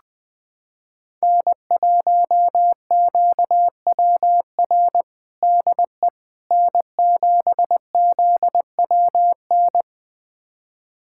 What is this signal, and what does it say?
Morse code, heard as a keyed tone, about 20 wpm, 700 Hz. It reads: N1QWR DE N7ZWN
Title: USA Callsign DE Callsign - 5 Characters - 20wpm